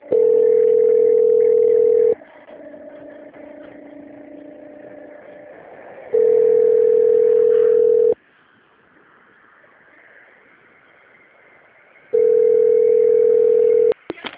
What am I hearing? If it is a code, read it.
Ringback tone (North America), 440 + 480 Hz.